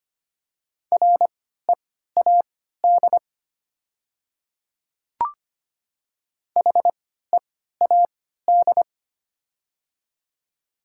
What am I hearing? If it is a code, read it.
Morse code, heard as a keyed tone, about 25 wpm, spaced as for 14 wpm, 700 Hz. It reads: READ HEAD
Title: 3x Character Spacing - Top 300 Words - 25wpm